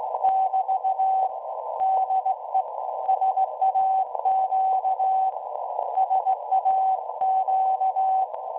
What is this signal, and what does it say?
Morse code, heard as a keyed tone, about 17 wpm, 760 Hz. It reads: <BT> DE SAQ SAQ
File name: WebSDR records / SAQ at PI4THT part od 1005z 17-2kHz